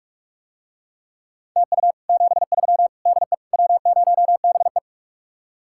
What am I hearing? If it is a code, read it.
Morse code, heard as a keyed tone, about 48 wpm, 700 Hz. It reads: TU 73 DE W0BE